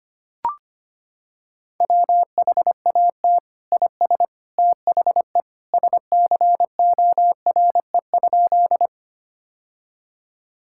Morse code, heard as a keyed tone, about 25 wpm, 700 Hz. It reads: WHAT IS THE SCORE?
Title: Sentences from Top 1600 Words 25wpm